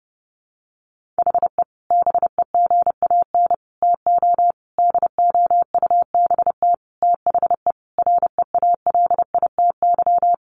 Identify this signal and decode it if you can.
Morse code, heard as a keyed tone, about 28 wpm, 700 Hz. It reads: HE BEGAN TO DOUBT THE REALITY
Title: Sentences from Top 1500 Words 30wpm